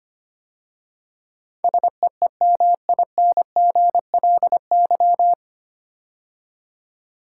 Morse code, heard as a keyed tone, about 25 wpm, 700 Hz. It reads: SEEMINGLY
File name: practice / 25wpm / 9 Letter Words - First 3 Letters Given - 25wpm